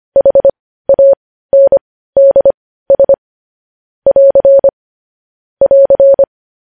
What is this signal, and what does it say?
Morse code, heard as a keyed tone, about 25 wpm, spaced as for 15 wpm, 550 Hz. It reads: HANDS <AR> <AR>
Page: News Headlines in Morse Code at 15 WPM